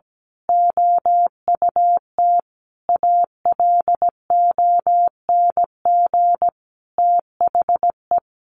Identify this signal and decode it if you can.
Morse code, heard as a keyed tone, about 17 wpm, 700 Hz. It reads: OUT ALONG THE
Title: A Sound Of Thunder 17wpm